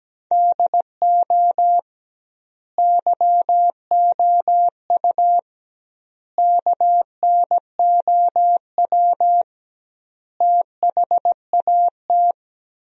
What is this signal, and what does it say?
Morse code, heard as a keyed tone, about 17 wpm, 700 Hz. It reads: DO YOU KNOW THAT
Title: 2x Word-Spacing - Sentences from Top 400 Words - 17wpm